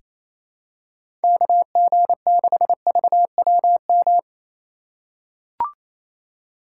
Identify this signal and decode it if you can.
Morse code, heard as a keyed tone, about 28 wpm, 700 Hz. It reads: KG6VWM
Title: USA Call Signs - 6 Characters - 2x Character Spacing - 28wpm